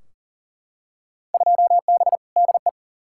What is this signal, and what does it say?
Morse code, heard as a keyed tone, about 40 wpm, 700 Hz. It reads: 2B DE
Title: Field Day Exchanges Encore - 40wpm